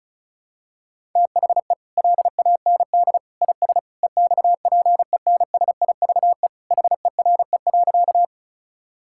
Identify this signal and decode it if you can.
Morse code, heard as a keyed tone, about 36 wpm, 700 Hz. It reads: THE LAND IS EXPENSIVE HERE.
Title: Sentences from Top 1000 Words - Review Part 6 - 35wpm